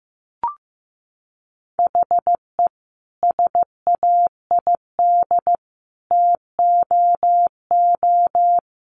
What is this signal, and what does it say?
Morse code, heard as a keyed tone, about 15 wpm, 700 Hz. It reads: HE SAID TOO
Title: Sentences from Top 200 Words 15wpm